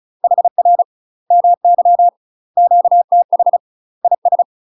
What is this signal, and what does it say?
Morse code, heard as a keyed tone, about 36 wpm, 700 Hz. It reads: HR MY QTH IS
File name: UK MASTER QSOs TRAINING / 35WPM / UK MASTER QSOs 0006